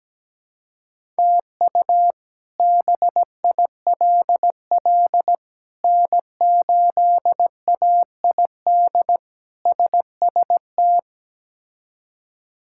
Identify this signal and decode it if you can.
Morse code, heard as a keyed tone, about 17 wpm, 700 Hz. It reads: TU BILL N8AID SST
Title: K1USN SST QSO Practice - 17wpm